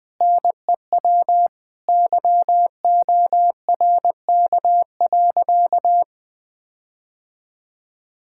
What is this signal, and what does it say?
Morse code, heard as a keyed tone, about 21 wpm, 700 Hz. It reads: NEW YORK.
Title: Sentences with 5-Word Phrases - Spoken Beginning - 20wpm